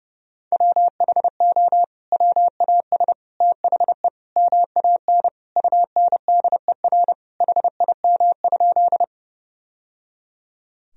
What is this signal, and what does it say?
Morse code, heard as a keyed tone, about 30 wpm, 700 Hz. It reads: WHO WAS THE MAN UNDER HIM?
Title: Sentences from Top 1000 Words - Review Part 1 - 30wpm